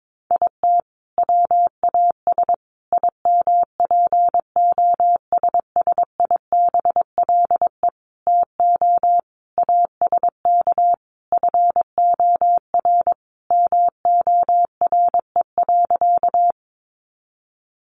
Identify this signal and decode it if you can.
Morse code, heard as a keyed tone, about 22 wpm, 700 Hz. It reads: IT WAS IMPOSSIBLE TO ASK FOR MORE.
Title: Sentences from Top 1100 Words 22wpm